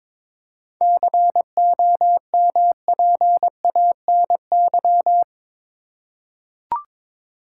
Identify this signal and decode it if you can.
Morse code, heard as a keyed tone, about 21 wpm, 700 Hz. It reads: COMPANY
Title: Seven Letter Words - 22wpm